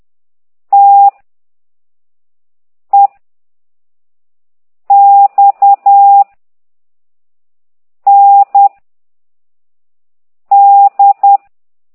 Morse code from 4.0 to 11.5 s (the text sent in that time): X N D